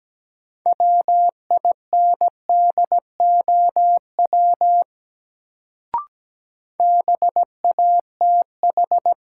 Morse code, heard as a keyed tone, about 17 wpm, 700 Hz. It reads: WINDOW BATH
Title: Sentences from Top 1600 Words - New Word Drill Alt - 17wpm